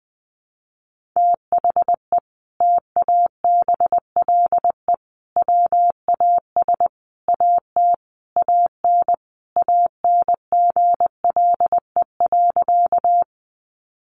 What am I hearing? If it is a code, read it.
Morse code, heard as a keyed tone, about 20 wpm, 700 Hz. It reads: THE TABLE WAS AT AN ANGLE.
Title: Sentences from Top 1500 Words 20wpm